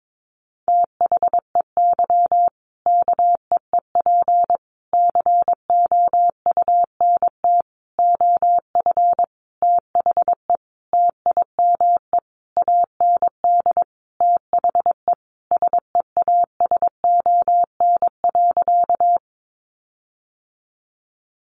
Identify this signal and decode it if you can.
Morse code, heard as a keyed tone, about 22 wpm, 700 Hz. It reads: THEY KEEP COUNT OF THE TIME AND THE SEASON.
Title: Sentences from Top 1500 Words 22wpm